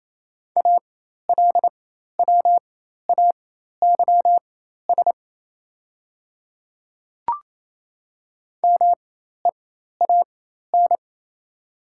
Morse code, heard as a keyed tone, about 28 wpm, spaced as for 13 wpm, 690 Hz. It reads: ALWAYS MEAN
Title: 4x Character Spacing - Top 300 Words - 28wpm